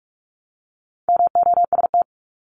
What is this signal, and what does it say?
Morse code, heard as a keyed tone, about 46 wpm, 700 Hz. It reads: MOST
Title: Multi-letter Phonemes 45wpm